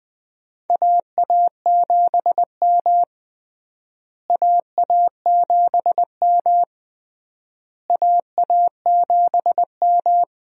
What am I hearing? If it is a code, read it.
Morse code, heard as a keyed tone, about 20 wpm, 700 Hz. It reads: AA7M AA7M AA7M
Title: USA Call Signs - 4 Characters - Repeated 3x - 20wpm